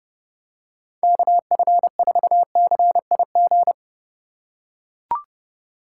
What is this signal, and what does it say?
Morse code, heard as a keyed tone, about 30 wpm, 700 Hz. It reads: KF4CIG